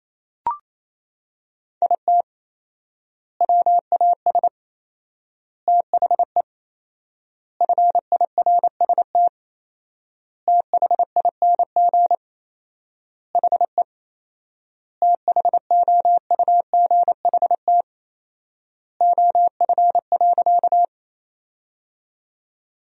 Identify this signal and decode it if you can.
Morse code, heard as a keyed tone, about 27 wpm, 700 Hz. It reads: IT WAS THE FIRST THING HE THOUGHT OF.
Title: 4x Word-Spacing - Sentences from Top 300 Words - 28wpm